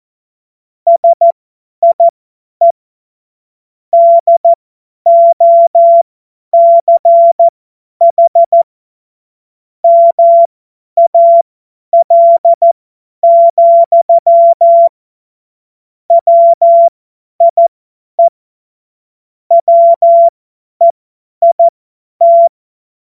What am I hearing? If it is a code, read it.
Morse code, heard as a keyed tone, about 14 wpm, spaced as for 10 wpm, 680 Hz. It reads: SIE DOCH MAL, WIE WEIT
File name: Morsekurs / Witze / Witz 30. Dezember 2025 50 F70